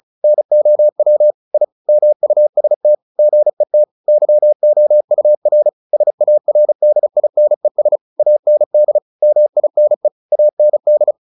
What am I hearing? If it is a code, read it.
Morse code, heard as a keyed tone, about 34 wpm, 600 Hz. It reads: NOW I MUST GET YOUR SARDINES AND MINE AND
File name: the old man and the sea / the old man and the sea 1 35wpm